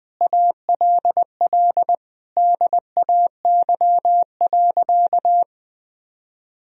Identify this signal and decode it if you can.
Morse code, heard as a keyed tone, about 20 wpm, 700 Hz. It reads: ALL DAY.
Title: Sentences from Top 1000 Words - Review Part 2 - 20wpm